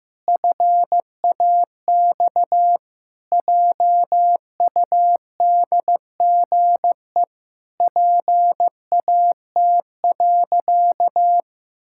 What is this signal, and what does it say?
Morse code, heard as a keyed tone, about 16 wpm, 700 Hz. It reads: FAX JUDGE PAT.
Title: Pangram Challenge v2 - 15wpm